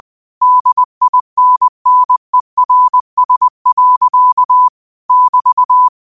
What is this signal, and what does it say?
Morse code, heard as a keyed tone, about 20 wpm, 1000 Hz. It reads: DINNERS. <BT>